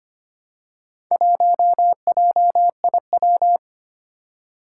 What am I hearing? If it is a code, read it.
Morse code, heard as a keyed tone, about 25 wpm, 700 Hz. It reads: 1JIW
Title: Four Challenging Letter-Number Combinations 25wpm